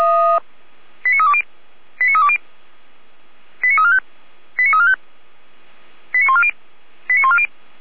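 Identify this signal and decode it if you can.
Five-tone selective call (ZVEI), heard as quick runs of five tones, tones of about 70 ms. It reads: 89320 89320 89345 89345 89140 89140
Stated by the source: ZVEI five-tone call received by SDR, archived by Feuerwehr Orlamünde.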